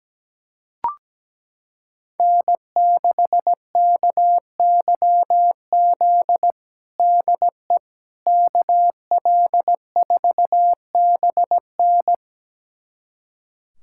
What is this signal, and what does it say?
Morse code, heard as a keyed tone, about 17 wpm, 700 Hz. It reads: N6KYZ DE KL4BN
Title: USA Callsign DE Callsign - 5 Characters - 17wpm